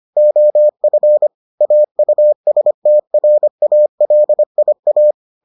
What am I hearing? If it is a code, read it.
Morse code, heard as a keyed tone, about 25 wpm, 600 Hz. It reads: OF AUSTRALIA